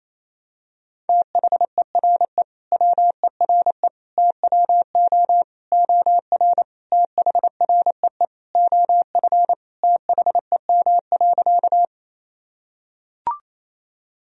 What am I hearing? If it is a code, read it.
Morse code, heard as a keyed tone, about 28 wpm, 700 Hz. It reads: THERE WERE TWO OR THREE OF THEM.
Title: Sentences from Top 1000 Words - Review Part 1 - 28wpm